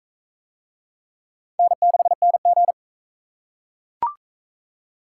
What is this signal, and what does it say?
Morse code, heard as a keyed tone, about 43 wpm, 700 Hz. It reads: N6NG